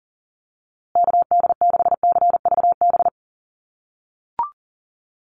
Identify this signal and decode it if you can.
Morse code, heard as a keyed tone, about 40 wpm, 700 Hz. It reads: KD6CVB
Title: USA Call Signs Repeated 3x 40wpm